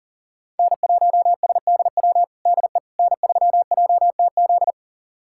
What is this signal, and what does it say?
Morse code, heard as a keyed tone, about 40 wpm, 700 Hz. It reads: N1SDW DE N3JTZ
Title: USA Callsign DE Callsign - 5 Characters - 40wpm